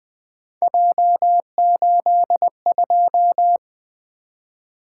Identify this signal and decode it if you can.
Morse code, heard as a keyed tone, about 20 wpm, 700 Hz. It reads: J82
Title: Three Letter-Number Combinations - 20wpm